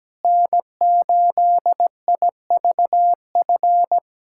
Morse code, heard as a keyed tone, about 17 wpm, 700 Hz. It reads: N8IVF
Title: USA Callsign DE Callsign 17wpm